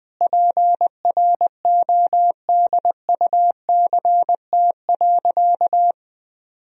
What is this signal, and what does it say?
Morse code, heard as a keyed tone, about 20 wpm, 700 Hz. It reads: PRODUCT.
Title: Sentences from Top 1000 Words - Review Part 3 - 20wpm